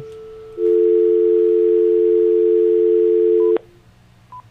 Dial tone (North America), 350 + 440 Hz.